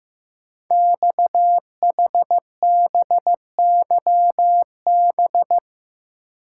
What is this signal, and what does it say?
Morse code, heard as a keyed tone, about 15 wpm, 700 Hz. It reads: XHBYB
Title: Code Groups - 5 Letters - 15wpm